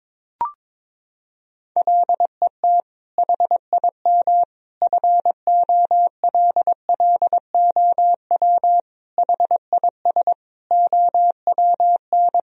Morse code, heard as a keyed tone, about 21 wpm, 700 Hz. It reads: LET HIM FOLLOW HIS OWN
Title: Sentences from Top 1000 Words - Review Part 1 - 22wpm